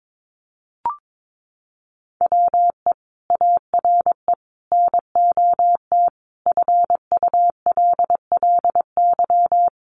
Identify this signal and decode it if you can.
Morse code, heard as a keyed tone, about 22 wpm, 700 Hz. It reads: WE ARE NOT FULLY
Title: Sentences from Top 1300 Words 22wpm